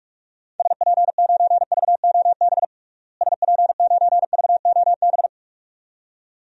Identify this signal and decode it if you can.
Morse code, heard as a keyed tone, about 46 wpm, 700 Hz. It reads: SP9VOB SP9VOB
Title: DXCC-100 Call Signs - 6 Characters - Repeated 2x - 45wpm